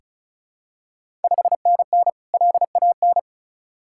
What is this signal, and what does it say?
Morse code, heard as a keyed tone, about 36 wpm, 690 Hz. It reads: 5NN LAN
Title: Ontario QSO Party Exchange 35wpm